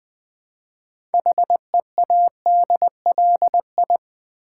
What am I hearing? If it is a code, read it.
Morse code, heard as a keyed tone, about 20 wpm, 700 Hz. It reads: HEADLI
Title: Build A Word - 3 Letters At A Time - 8-Letter Words - 20wpm